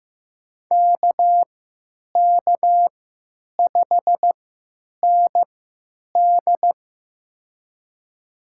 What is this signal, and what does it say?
Morse code, heard as a keyed tone, about 15 wpm, spaced as for 9 wpm, 700 Hz. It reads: KK5ND
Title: USA Call Signs - 5 Characters - 3x Character Spacing - 15wpm